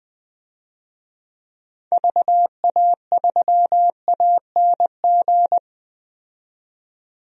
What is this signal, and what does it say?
Morse code, heard as a keyed tone, about 21 wpm, 700 Hz. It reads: VA3ANG
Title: DXCC-100 Call Signs - 6 Characters - 20wpm